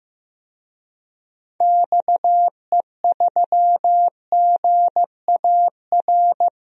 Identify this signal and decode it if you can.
Morse code, heard as a keyed tone, about 15 wpm, 700 Hz. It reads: XE3GAR
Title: DXCC-100 Call Signs - 6 Characters - 15wpm